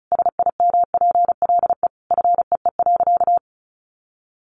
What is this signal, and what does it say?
Morse code, heard as a keyed tone, about 34 wpm, 700 Hz. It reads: SIMPLE FEE.
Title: Sentences from Top 1500 Words 35wpm